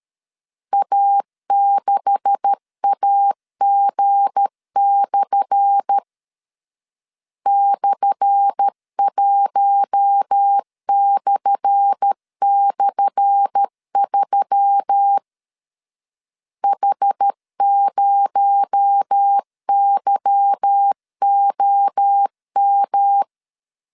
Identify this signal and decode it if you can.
Morse code, heard as a keyed tone, about 13 wpm, 800 Hz. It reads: A6AG/ /1//3 H0YOM
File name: LEZIONI / 10 SIGNS / 10A SLASH 12wpm 2314